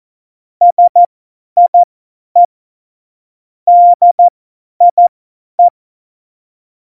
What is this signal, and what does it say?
Morse code, heard as a keyed tone, about 14 wpm, spaced as for 10 wpm, 700 Hz. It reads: SIE DIE
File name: Morsekurs / Witze / Witz 07. Mai 2019, 50 F70